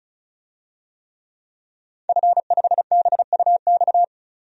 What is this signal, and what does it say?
Morse code, heard as a keyed tone, about 36 wpm, 700 Hz. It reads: F5BUX
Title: DXCC-100 Call Signs - 5 Characters - 35wpm